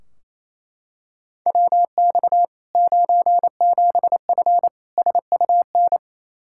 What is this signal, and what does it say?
Morse code, heard as a keyed tone, about 28 wpm, 700 Hz. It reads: WX 97F SUN
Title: Practice QSO - Exchange 2 - Info - 28wpm